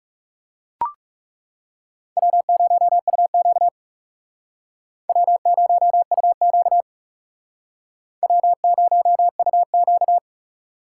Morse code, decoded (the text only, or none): W0UQ W0UQ W0UQ